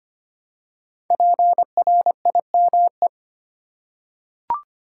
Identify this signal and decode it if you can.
Morse code, heard as a keyed tone, about 25 wpm, 700 Hz. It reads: PRIME
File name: 5 Letter Words - First 3 Letters Given - 25wpm